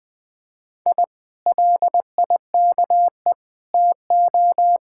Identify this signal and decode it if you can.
Morse code, heard as a keyed tone, about 20 wpm, 700 Hz. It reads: I LIKE TO
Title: Speed-Racing Common QSO Phrases Encore - 17wpm